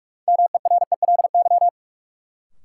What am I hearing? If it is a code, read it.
Morse code, heard as a keyed tone, about 46 wpm, 700 Hz. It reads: MERELY
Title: Sentences from Top 1400 Words - New Word Drill - 45wpm